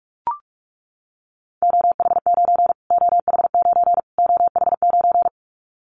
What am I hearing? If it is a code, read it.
Morse code, heard as a keyed tone, about 44 wpm, 700 Hz. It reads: OH9 OH9 OH9
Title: Callsign Prefixes for Top 25 DXCC - Repeated 3x - 45wpm